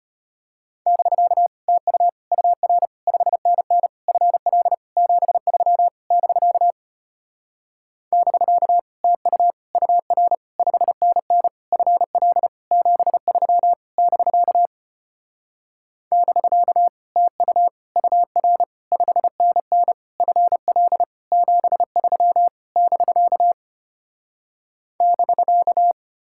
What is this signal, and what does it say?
Morse code, heard as a keyed tone, about 33 wpm, 700 Hz. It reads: <BK> TU UR 5NN FL 73 <BK> <BK> TU UR 5NN FL 73 <BK> <BK> TU UR 5NN FL 73 <BK> <BK>